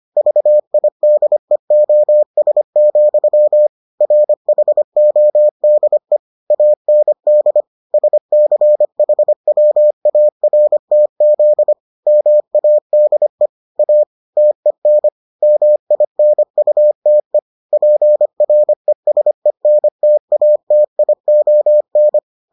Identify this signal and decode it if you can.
Morse code, heard as a keyed tone, about 25 wpm, 600 Hz. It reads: VIDEOS, RHODE AND SCHWARTZ MADE A TEN MINUTE PRESENTATION